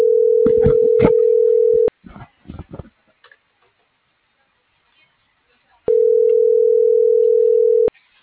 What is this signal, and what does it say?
Ringback tone (North America), 440 + 480 Hz.
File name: CENTRAL GENERAL / out michelle.carrera